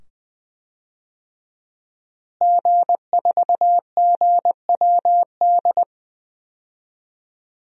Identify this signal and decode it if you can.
Morse code, heard as a keyed tone, about 21 wpm, 700 Hz. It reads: G4GWD
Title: DXCC-100 Call Signs - 5 Characters - Repeated 2x - 20wpm